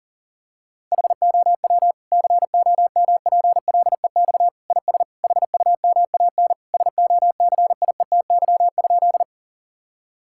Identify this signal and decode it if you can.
Morse code, heard as a keyed tone, about 40 wpm, 700 Hz. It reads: HOW COMPLEX IS HUMAN SOCIETY?
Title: Sentences from Top 1000 Words - Review Part 6 - 40wpm